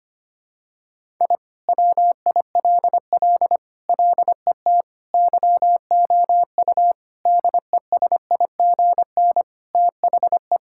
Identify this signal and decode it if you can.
Morse code, heard as a keyed tone, about 25 wpm, 700 Hz. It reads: I WILL LET YOU DESIGN THE
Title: Sentences from Top 1000 Words - Review Part 5 - 25wpm